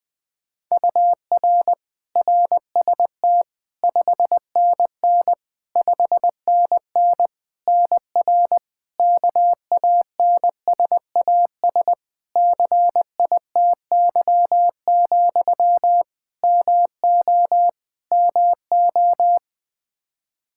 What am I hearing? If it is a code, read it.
Morse code, heard as a keyed tone, about 20 wpm, 700 Hz. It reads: UR RST 5NN 5NN NR KANSAS CITY, MO MO
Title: Practice QSO - Exchange 1 - Beginning - 20wpm